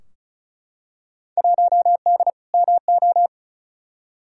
Morse code, heard as a keyed tone, about 36 wpm, 700 Hz. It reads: 1D MO
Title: Field Day Exchanges - 35wpm